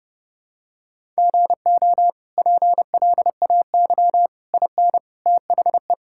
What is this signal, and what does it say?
Morse code, heard as a keyed tone, about 30 wpm, 700 Hz. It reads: GO PLAY IN THE
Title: Sentences from Top 1000 Words - Review Part 5 - 30wpm